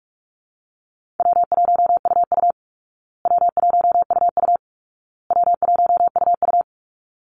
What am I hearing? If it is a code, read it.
Morse code, heard as a keyed tone, about 44 wpm, 700 Hz. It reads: W1UU W1UU W1UU
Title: Easy Call Signs Repeated 3x - 45wpm